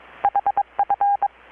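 Morse code, heard as a keyed tone, about 22 wpm, 810 Hz. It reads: HF
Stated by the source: Morse from KPH's Night of Nights XX maritime radiotelegraphy event, received in Chiapas, Mexico about 3,800 km away on a Kenwood R-600 with a grounded 10 m long-wire antenna.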